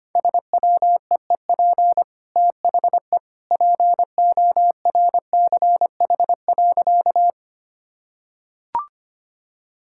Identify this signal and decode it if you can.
Morse code, heard as a keyed tone, about 25 wpm, 700 Hz. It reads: SWEEP THE PORCH.
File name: Easy Sentences with Hints - 25wpm